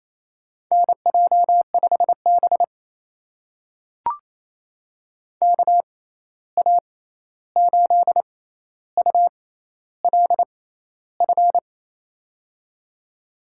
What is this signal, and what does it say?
Morse code, heard as a keyed tone, about 28 wpm, spaced as for 13 wpm, 700 Hz. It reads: NJ5B KA8ULF
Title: USA Call Signs - 6x Character Spacing - 28wpm